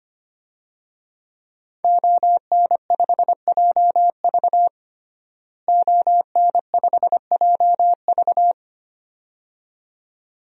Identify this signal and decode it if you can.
Morse code, heard as a keyed tone, about 25 wpm, 700 Hz. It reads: ON5JV ON5JV
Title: DXCC-100 Call Signs - 5 Characters - Repeated 2x - 25wpm